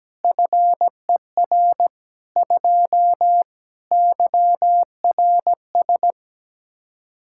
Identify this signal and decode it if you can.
Morse code, heard as a keyed tone, about 17 wpm, 700 Hz. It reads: FER 2 YRS
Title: Practice QSO - Exchange 3 - More Info - 17wpm